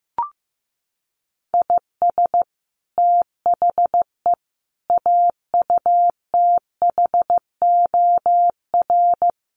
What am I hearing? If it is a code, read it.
Morse code, heard as a keyed tone, about 15 wpm, 700 Hz. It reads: IS THE AUTHOR
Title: Sets of 4 Words 15wpm